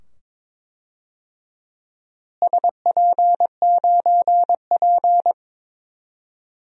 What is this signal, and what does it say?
Morse code, heard as a keyed tone, about 22 wpm, 700 Hz. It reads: SP9P
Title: DXCC-100 Call Signs - 4 Characters - Repeated 2x - 22wpm